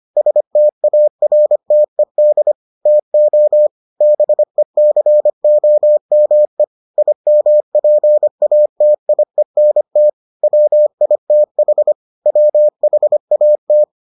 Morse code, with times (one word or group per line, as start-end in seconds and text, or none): 0.2-2.5 STARTED
2.9-3.7 TO
4.0-6.6 BECOME
7.0-10.1 IMPATIENT
10.4-11.9 WITH
12.3-13.8 WHAT